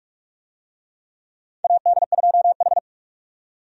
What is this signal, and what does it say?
Morse code, heard as a keyed tone, about 44 wpm, 700 Hz. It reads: AD2H
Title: USA Call Signs - 4 Characters - 45wpm